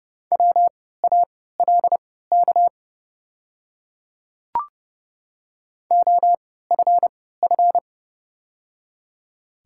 Morse code, heard as a keyed tone, about 30 wpm, spaced as for 17 wpm, 700 Hz. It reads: WALK OFF
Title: 3x Character Spacing - Top 300 Words - 30wpm